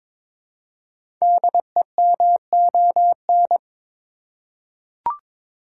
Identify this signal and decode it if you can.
Morse code, heard as a keyed tone, about 22 wpm, 700 Hz. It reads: DEMON
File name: 5 Letter Words - First 3 Letters Given - 22wpm